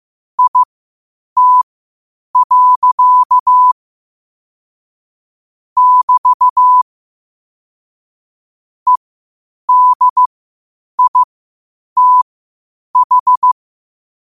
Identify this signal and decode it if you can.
Morse code, heard as a keyed tone, about 15 wpm, spaced as for 8 wpm, 1000 Hz. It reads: IT. <BT> EDITH
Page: 8 wpm morse code quotes